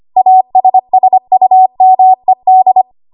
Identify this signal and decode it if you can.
Morse code, heard as a keyed tone, about 25 wpm, 750 Hz. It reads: ASSUMED
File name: Archive / 25 WPM / 25WPM